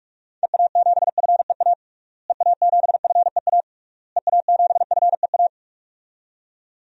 Morse code, heard as a keyed tone, about 46 wpm, 700 Hz. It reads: EA7FEA EA7FEA EA7FEA
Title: DXCC-100 Call Signs - 6 Characters - Repeated 3x - 45wpm